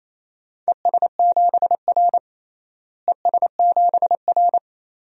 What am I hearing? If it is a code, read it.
Morse code, heard as a keyed tone, about 28 wpm, 700 Hz. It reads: ES7R ES7R
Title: DXCC-100 Call Signs - 4 Characters - Repeated 2x - 28wpm